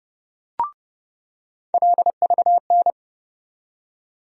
Morse code, heard as a keyed tone, about 30 wpm, 700 Hz. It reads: LVN
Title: USA Call Sign Postfixes - 3 Letters 30wpm